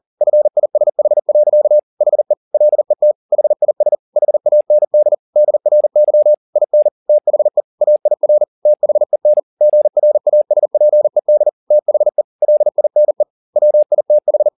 Morse code, heard as a keyed tone, about 40 wpm, 600 Hz. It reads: FISH. HE LET HIS HAND DRY IN THE AIR THEN GRASPED THE LINE WITH
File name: the old man and the sea 2 40wpm